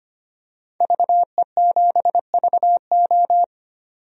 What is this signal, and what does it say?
Morse code, heard as a keyed tone, about 25 wpm, 700 Hz. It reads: VE7VO
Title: DXCC-100 Call Signs - 5 Characters - 25wpm